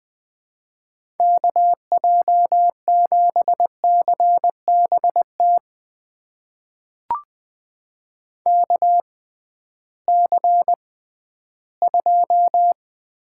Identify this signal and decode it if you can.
Morse code, heard as a keyed tone, about 20 wpm, spaced as for 13 wpm, 700 Hz. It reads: KJ7CBT KC2
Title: USA Call Signs - 6 Characters - 6x Character Spacing - 20wpm